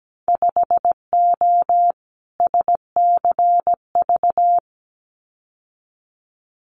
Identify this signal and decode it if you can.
Morse code, heard as a keyed tone, about 17 wpm, 700 Hz. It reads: 5O SCV
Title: Winter Field Day Exchanges Encore 17wpm